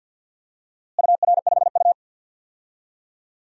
Morse code, heard as a keyed tone, about 47 wpm, 700 Hz. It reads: UR5U
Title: DXCC-100 Call Signs - 4 Characters - Repeated 3x - 50wpm